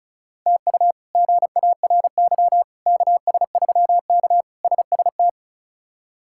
Morse code, heard as a keyed tone, about 36 wpm, 700 Hz. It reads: TU GARY KS3K SST
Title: K1USN SST QSO Practice - 35wpm